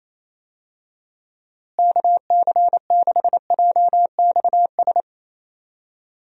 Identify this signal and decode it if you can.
Morse code, heard as a keyed tone, about 28 wpm, 700 Hz. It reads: KC6JXS
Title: USA Call Signs - 6 Characters - Repeated 2x - 28wpm